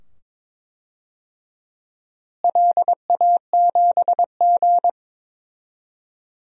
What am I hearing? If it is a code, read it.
Morse code, heard as a keyed tone, about 22 wpm, 700 Hz. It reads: LA7G
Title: DXCC-100 Call Signs - 4 Characters - Repeated 2x - 22wpm